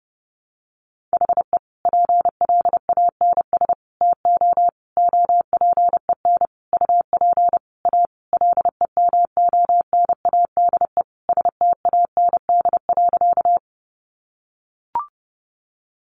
Morse code, heard as a keyed tone, about 30 wpm, 700 Hz. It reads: HE PLANS TO OPEN UP A LEMONADE STAND.
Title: Additional Easy Phrases 30wpm